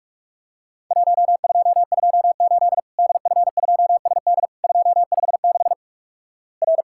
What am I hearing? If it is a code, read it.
Morse code, heard as a keyed tone, about 48 wpm, 700 Hz. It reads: 1228 DF2SD 256 R
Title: Worked All Europe DX Contest - QTC Exchange - 45wpm